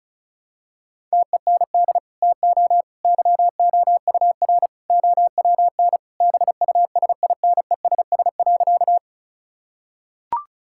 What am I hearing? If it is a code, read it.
Morse code, heard as a keyed tone, about 34 wpm, 700 Hz. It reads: TEND TO YOUR OWN BUSINESS.
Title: Sentences from Top 1000 Words - Review Part 5 - 35wpm